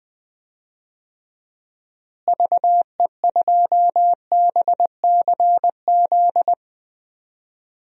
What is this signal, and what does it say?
Morse code, heard as a keyed tone, about 20 wpm, 700 Hz. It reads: VE2BCZ
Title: DXCC-100 Call Signs - 6 Characters - Repeated 2x - 20wpm